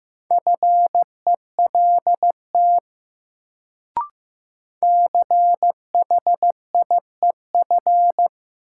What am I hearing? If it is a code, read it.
Morse code, heard as a keyed tone, about 15 wpm, 700 Hz. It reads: FELT CHIEF
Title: Sets of 5 Words Encore - New Word Drill - 15wpm